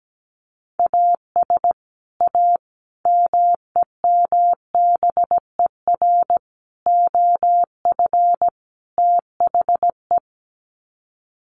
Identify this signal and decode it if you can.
Morse code, heard as a keyed tone, about 17 wpm, 700 Hz. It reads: AS A MEMBER OF THE
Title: Sets of 5 Words 17wpm